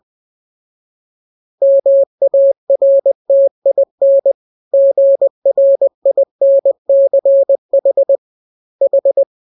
Morse code, heard as a keyed tone, about 20 wpm, 550 Hz. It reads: MARTIN GRINCH H